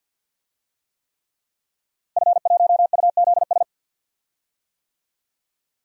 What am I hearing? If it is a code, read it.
Morse code, heard as a keyed tone, about 52 wpm, 700 Hz. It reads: F1UZS
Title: DXCC-100 Call Signs - 5 Characters - Repeated 3x - 50wpm